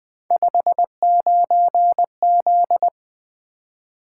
Morse code, heard as a keyed tone, about 20 wpm, 700 Hz. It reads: 59Z
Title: Three Letter-Number Combinations - 20wpm